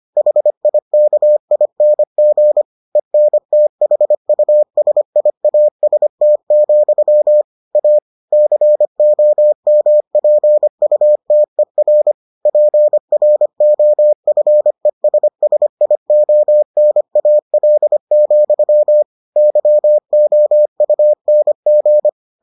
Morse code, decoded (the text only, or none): HIKING ENTHUSIAST, A COMPUTER PROFESSIONAL, YOUNG